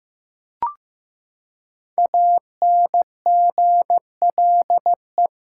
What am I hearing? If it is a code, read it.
Morse code, heard as a keyed tone, about 15 wpm, 700 Hz. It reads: ANGLE
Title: Five Letter Words - 15wpm